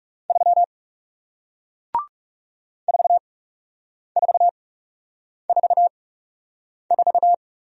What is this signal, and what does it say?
Morse code, heard as a keyed tone, about 40 wpm, spaced as for 8 wpm, 700 Hz. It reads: 3 4444